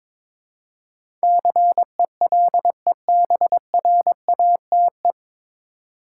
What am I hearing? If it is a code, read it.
Morse code, heard as a keyed tone, about 22 wpm, 700 Hz. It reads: CELEBRATE
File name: 9 Letter Words - First 3 Letters Given - 22wpm